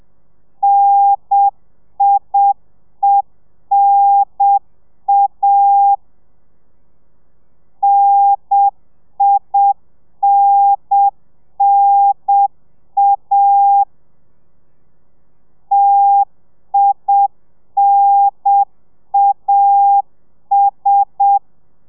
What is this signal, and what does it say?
Morse code, heard as a keyed tone, about 7 wpm, 800 Hz. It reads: NIENA NINNA TINAS